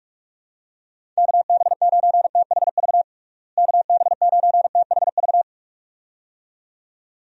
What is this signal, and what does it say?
Morse code, heard as a keyed tone, about 45 wpm, 700 Hz. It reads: KB9THV KB9THV
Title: USA Call Signs - 6 Characters - Repeated 2x - 45wpm